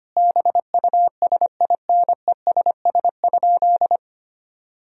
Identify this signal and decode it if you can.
Morse code, heard as a keyed tone, about 25 wpm, 700 Hz. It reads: BUSINESS?
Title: Sentences from Top 1000 Words - Review Part 4 - 25wpm